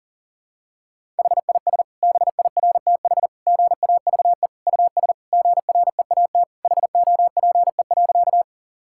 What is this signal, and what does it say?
Morse code, heard as a keyed tone, about 40 wpm, 700 Hz. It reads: HIS BIRTH GAVE US GREAT HOPE.
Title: Sentences from Top 1000 Words - Review Part 5 - 40wpm